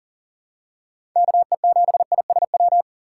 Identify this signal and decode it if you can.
Morse code, heard as a keyed tone, about 40 wpm, 700 Hz. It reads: KE7ISW
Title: USA Call Signs - 3x Character Spacing - 40wpm